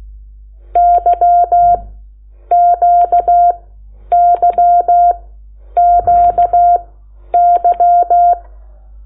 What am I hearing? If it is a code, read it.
Morse code, heard as a keyed tone, about 16 wpm, spaced as for 9 wpm, 700 Hz. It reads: YQYQY
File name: Lecciones / CLASE Nº8